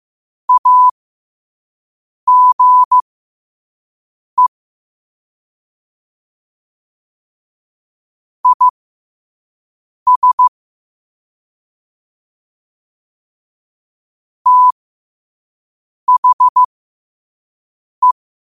Morse code, heard as a keyed tone, about 15 wpm, spaced as for 5 wpm, 1000 Hz. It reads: AGE IS THE
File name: qotdfiles / 5wpm